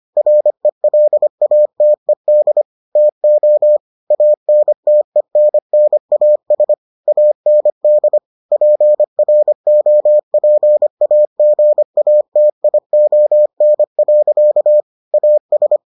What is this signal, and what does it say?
Morse code, heard as a keyed tone, about 25 wpm, 600 Hz. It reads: RELATED TO ANTENNAS AND PROPAGATION. AS